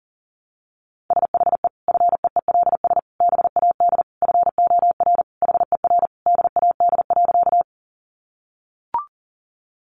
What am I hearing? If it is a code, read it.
Morse code, heard as a keyed tone, about 40 wpm, 700 Hz. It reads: SHE FEELS BAD FOR HER DAD.
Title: Additional Easy Phrases 40wpm